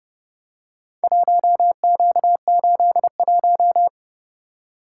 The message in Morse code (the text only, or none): 1Q81